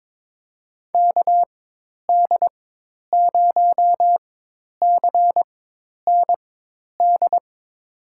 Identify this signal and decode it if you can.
Morse code, heard as a keyed tone, about 22 wpm, spaced as for 10 wpm, 700 Hz. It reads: KD0CND
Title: USA Call Signs - 6 Characters - 4x Character Spacing - 22wpm